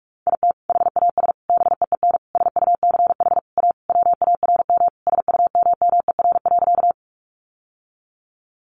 Morse code, heard as a keyed tone, about 44 wpm, 700 Hz. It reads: IT HAS BEEN SUCH A WARM SUMMER.
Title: Sentences from Top 1100 Words 45wpm